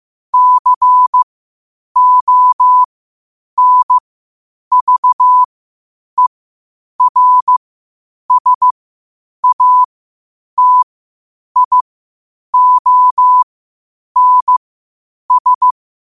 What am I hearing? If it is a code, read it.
Morse code, heard as a keyed tone, about 15 wpm, spaced as for 8 wpm, 1000 Hz. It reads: CONVERSATIONS
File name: qotdfiles / 8wpm